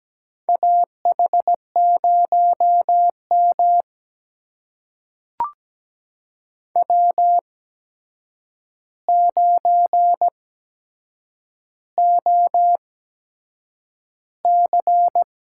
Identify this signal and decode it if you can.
Morse code, heard as a keyed tone, about 17 wpm, spaced as for 7 wpm, 700 Hz. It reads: AH0M W9OC
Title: USA Call Signs - 4 Characters - 8x Character Spacing - 17wpm